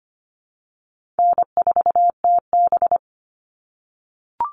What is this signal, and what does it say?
Morse code, heard as a keyed tone, about 25 wpm, 700 Hz. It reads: N4TB
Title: Easy Call Signs - 25wpm